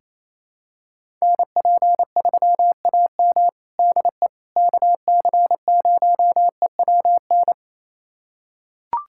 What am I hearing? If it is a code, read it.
Morse code, heard as a keyed tone, about 28 wpm, 700 Hz. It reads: NP3AM DE KC0EWN
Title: USA Callsign DE Callsign 28wpm